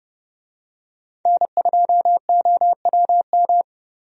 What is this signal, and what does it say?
Morse code, heard as a keyed tone, about 30 wpm, 700 Hz. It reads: N2OWM
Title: USA Call Signs - 5 Characters - Repeated 3x - 30wpm